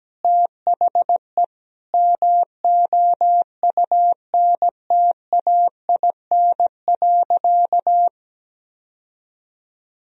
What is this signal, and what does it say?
Morse code, heard as a keyed tone, about 17 wpm, 700 Hz. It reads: THE MOUNTAIN.
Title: Sentences from Top 1000 Words - Review Part 1 - 17wpm